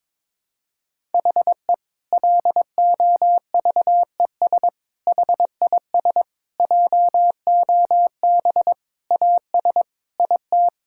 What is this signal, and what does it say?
Morse code, heard as a keyed tone, about 22 wpm, 700 Hz. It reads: HE LOVES HIS JOB AS IT
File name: Difficult Sentences with Hints - 22wpm